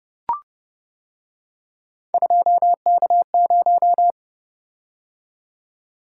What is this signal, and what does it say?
Morse code, heard as a keyed tone, about 30 wpm, 700 Hz. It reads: 2K0